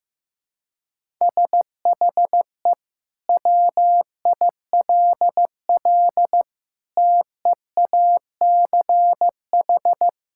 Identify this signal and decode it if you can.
Morse code, heard as a keyed tone, about 15 wpm, 700 Hz. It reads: SHE WILL TEACH
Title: Sentences from Top 1000 Words - Review Part 6 - 15wpm